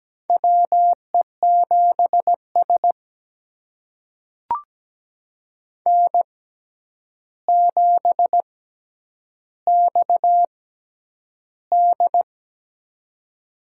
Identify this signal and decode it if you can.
Morse code, heard as a keyed tone, about 17 wpm, spaced as for 9 wpm, 700 Hz. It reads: WE7S N7XD
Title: USA Call Signs - 4 Characters - 6x Character Spacing - 17wpm